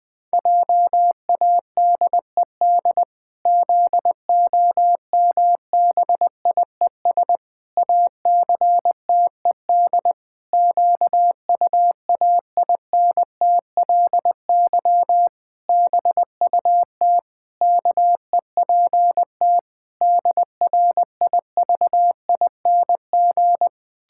Morse code, heard as a keyed tone, about 20 wpm, 700 Hz. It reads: JADED ZOMBIES ACTED QUAINTLY BUT KEPT DRIVING
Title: Pangram Challenge v2 - 20wpm